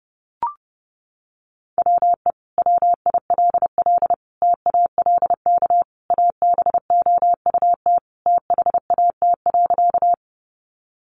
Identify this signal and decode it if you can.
Morse code, heard as a keyed tone, about 30 wpm, 700 Hz. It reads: WE WILL TALK ABOUT THAT.
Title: Sentences from Top 400 Words 30wpm